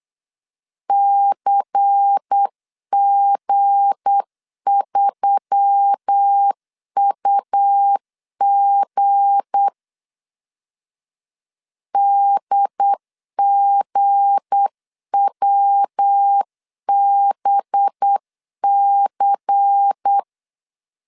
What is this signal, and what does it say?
Morse code, heard as a keyed tone, about 8 wpm, 800 Hz. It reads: CG3UG DGWBC
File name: LEZIONI / 7 GF9Z / 7 G 8wpm 3118